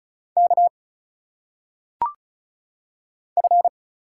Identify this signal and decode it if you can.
Morse code, heard as a keyed tone, about 35 wpm, 700 Hz. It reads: K F
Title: Single Letters - Phonetic Pronunciation 35wpm